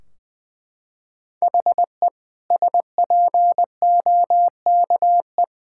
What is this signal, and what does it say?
Morse code, heard as a keyed tone, about 20 wpm, 700 Hz. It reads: HE SPOKE
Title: Sentences from Top 1600 Words 20wpm